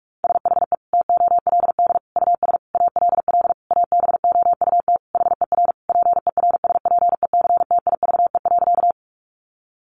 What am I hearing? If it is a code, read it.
Morse code, heard as a keyed tone, about 44 wpm, 700 Hz. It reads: SHE TOLD US ALL ABOUT HER PERSPECTIVE.
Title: Sentences from Top 1200 Words 45wpm